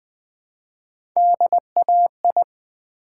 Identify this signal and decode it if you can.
Morse code, heard as a keyed tone, about 20 wpm, 700 Hz. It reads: DAI
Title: Build A Word - 1 Letter At A Time - 5-Letter Words - 20wpm